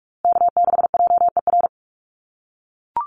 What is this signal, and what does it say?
Morse code, heard as a keyed tone, about 44 wpm, 700 Hz. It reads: K6JER
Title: USA Call Signs Repeated 3x 45wpm